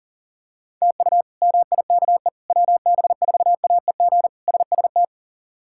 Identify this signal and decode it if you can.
Morse code, heard as a keyed tone, about 37 wpm, 700 Hz. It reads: TU MIKE WB4AEG SST
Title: K1USN SST QSO Practice - 40wpm